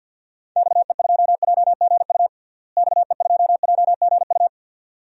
Morse code, heard as a keyed tone, about 46 wpm, 700 Hz. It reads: XE2JGU XE2JGU
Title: DXCC-100 Call Signs - 6 Characters - Repeated 2x - 50wpm